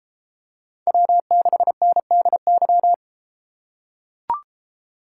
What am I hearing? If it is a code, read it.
Morse code, heard as a keyed tone, about 31 wpm, 700 Hz. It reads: W6NDY